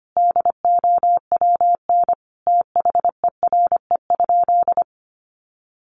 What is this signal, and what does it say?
Morse code, heard as a keyed tone, about 25 wpm, 700 Hz. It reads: DOWN THERE?
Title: Sentences from Top 1100 Words 25wpm